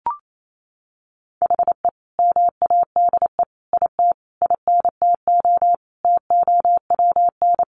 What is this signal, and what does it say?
Morse code, heard as a keyed tone, about 28 wpm, 700 Hz. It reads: HE MADE IT INTO TOWN
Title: Sentences from Top 1300 Words 28wpm